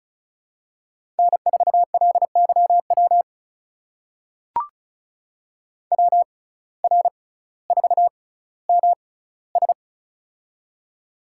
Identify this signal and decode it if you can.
Morse code, heard as a keyed tone, about 36 wpm, spaced as for 18 wpm, 700 Hz. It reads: N4LYW WR4MS
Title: USA Call Signs - 6x Character Spacing - 35wpm